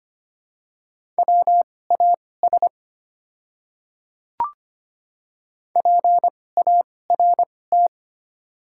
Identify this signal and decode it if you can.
Morse code, heard as a keyed tone, about 25 wpm, spaced as for 18 wpm, 700 Hz. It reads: WAS PART
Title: 2x Character Spacing - Top 500 Words - 25wpm